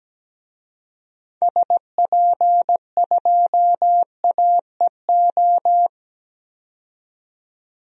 Morse code, heard as a keyed tone, about 17 wpm, 700 Hz. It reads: SP2AEO
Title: DXCC-100 Call Signs - 6 Characters - 17wpm